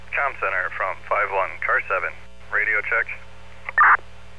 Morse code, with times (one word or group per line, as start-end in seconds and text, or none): none